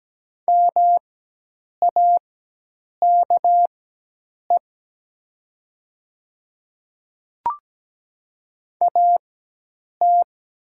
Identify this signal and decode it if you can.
Morse code, heard as a keyed tone, about 17 wpm, spaced as for 8 wpm, 700 Hz. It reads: MAKE AT
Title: 4x Character Spacing - Top 100 Words - 17wpm